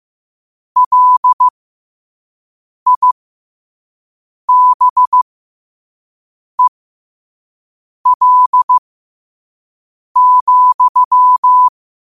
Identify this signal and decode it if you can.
Morse code, heard as a keyed tone, about 15 wpm, spaced as for 5 wpm, 1000 Hz. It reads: LIBEL,
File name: qotdfiles / 5wpm